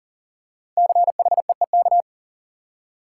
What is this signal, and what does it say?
Morse code, heard as a keyed tone, about 38 wpm, 700 Hz. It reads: CHEEK
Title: Five Letter Words - 40wpm